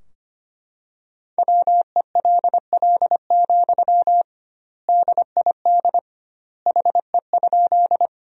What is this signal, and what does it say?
Morse code, heard as a keyed tone, about 25 wpm, 700 Hz. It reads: WELL, DID HE?
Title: 2x Word-Spacing - Sentences from Top 200 Words - 25wpm